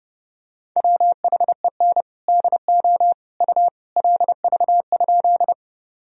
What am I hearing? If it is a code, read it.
Morse code, heard as a keyed tone, about 31 wpm, 700 Hz. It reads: WHEN DO U LV?
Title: Sentences from Top 400 Words and Abbreviations 30wpm